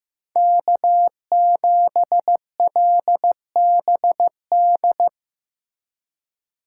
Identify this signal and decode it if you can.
Morse code, heard as a keyed tone, about 17 wpm, 700 Hz. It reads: K7LBD
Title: Speed Racing - USA 1x3 Call Signs 15wpm